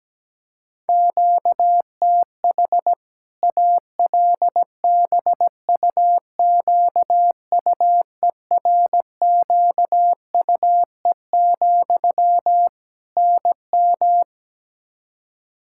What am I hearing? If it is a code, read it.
Morse code, heard as a keyed tone, about 17 wpm, 700 Hz. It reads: QTH ALBUQUERQUE, NM
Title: Practice QSO - Exchange 1 - Beginning - 17wpm